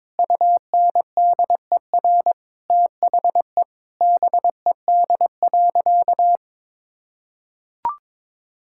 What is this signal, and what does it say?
Morse code, heard as a keyed tone, about 22 wpm, 700 Hz. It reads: UNDER THE BED.
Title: Sentences from Top 1000 Words - Review Part 6 - 22wpm